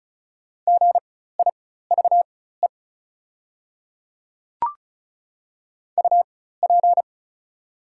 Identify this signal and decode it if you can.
Morse code, heard as a keyed tone, about 36 wpm, spaced as for 16 wpm, 700 Hz. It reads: GIVE UP
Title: 4x Character Spacing - Top 200 Words - 35wpm